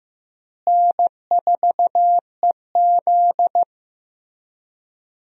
Morse code, heard as a keyed tone, about 15 wpm, 700 Hz. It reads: N4EZ
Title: USA Call Signs - 4 Characters - Repeated 3x - 15wpm